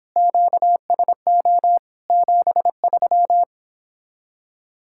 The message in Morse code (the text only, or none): QSO 73